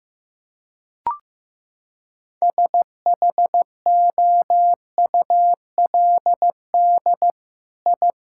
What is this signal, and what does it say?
Morse code, heard as a keyed tone, about 15 wpm, 700 Hz. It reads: SHOULD I
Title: Sentences from Top 1000 Words - Review Part 6 - 15wpm